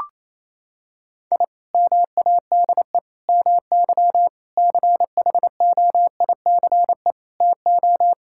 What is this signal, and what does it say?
Morse code, heard as a keyed tone, about 28 wpm, 700 Hz. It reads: I MADE MY CHOICE TO
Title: Sentences from Top 1000 Words - Review Part 6 - 28wpm